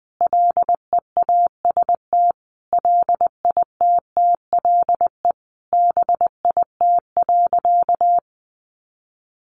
Morse code, heard as a keyed tone, about 20 wpm, 700 Hz. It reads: LEAST LITTLE BIT.
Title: Sentences from Top 900 Words 20wpm